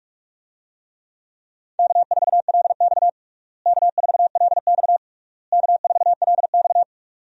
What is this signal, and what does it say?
Morse code, heard as a keyed tone, about 41 wpm, 700 Hz. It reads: K4LX K4LX K4LX
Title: USA Call Signs - 4 Characters - Repeated 3x - 45wpm